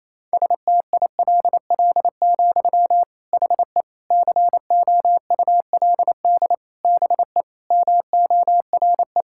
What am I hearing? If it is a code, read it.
Morse code, heard as a keyed tone, about 28 wpm, 700 Hz. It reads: STILL, HE COULD BE MORE
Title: Sentences from Top 1000 Words - Review Part 5 - 28wpm